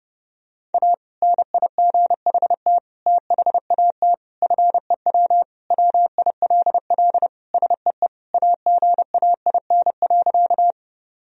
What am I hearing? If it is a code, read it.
Morse code, heard as a keyed tone, about 32 wpm, 700 Hz. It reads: A NIGHT THAT FEW WILL SEE AGAIN.
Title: Sentences from Top 1000 Words - Review Part 1 - 30wpm